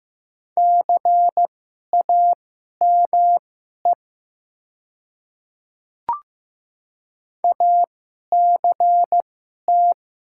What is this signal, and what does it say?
Morse code, heard as a keyed tone, about 15 wpm, spaced as for 11 wpm, 700 Hz. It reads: CAME ACT
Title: 2x Character Spacing - Top 200 Words - 15wpm